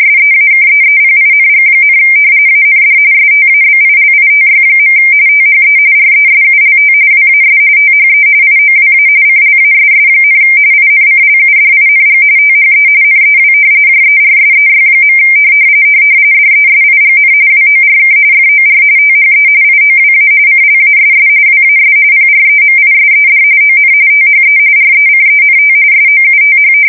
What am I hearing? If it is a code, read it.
RTTY, about 45.45 baud. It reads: SEAKEEPING, AND STABILITY. THE RESULT WAS THE IOWA CLASS, INCLUDING THE NEW JERSEY. DESIGN WORK FOR THE IOWA CLASS WAS PREMISED ON THE RECENT SOUTH
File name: fsk170 / battleshipnj